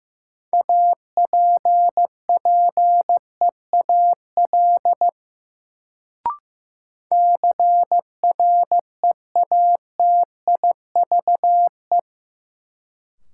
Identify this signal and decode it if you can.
Morse code, heard as a keyed tone, about 15 wpm, 700 Hz. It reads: APPEAL CREATIVE
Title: Sentences from Top 1300 Words - New Word Drill Alt - 15wpm